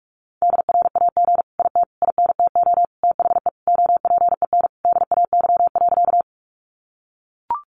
Morse code, heard as a keyed tone, about 44 wpm, 700 Hz. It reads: DRAG IT INTO THE OPEN DAY.
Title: Sentences from Top 1500 Words 45wpm